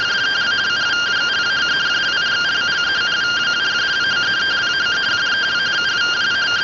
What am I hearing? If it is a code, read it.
Two-tone teleprinter signal, about 170 Hz shift.